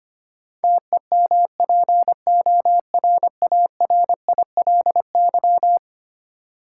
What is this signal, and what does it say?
Morse code, heard as a keyed tone, about 25 wpm, 700 Hz. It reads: TEMPORARILY
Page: morse-code-ninja-practice-25wpm